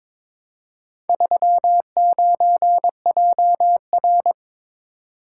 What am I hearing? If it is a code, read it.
Morse code, heard as a keyed tone, about 22 wpm, 700 Hz. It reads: 39JR
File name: practice / 22wpm / Four Letter-Number Combinations - 22wpm